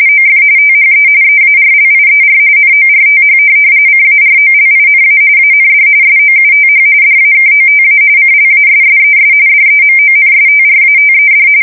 RTTY, about 45.45 baud. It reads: OF THE EARLIER NORTH CAROLINA AND SOUTH DAKOTA CLASSES DEMONSTRATED
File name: fsk170 / USS-Missouri